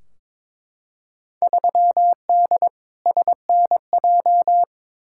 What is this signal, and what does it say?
Morse code, heard as a keyed tone, about 22 wpm, 700 Hz. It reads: 3D SNJ
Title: Field Day Exchanges Encore - 22wpm